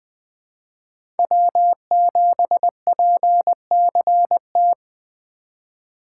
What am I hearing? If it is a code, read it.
Morse code, heard as a keyed tone, about 20 wpm, 700 Hz. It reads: W7PCT
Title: 2023 POTA Activator Call Signs - 20wpm